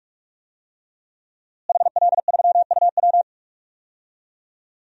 Morse code, heard as a keyed tone, about 46 wpm, 700 Hz. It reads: HL3UW
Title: DXCC-100 Call Signs - 5 Characters - Repeated 2x - 45wpm